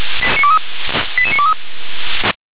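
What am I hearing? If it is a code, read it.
Five-tone selective call (ZVEI), heard as quick runs of five tones, tones of about 70 ms. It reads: 88923
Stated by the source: ZVEI five-tone selective call received by SDR.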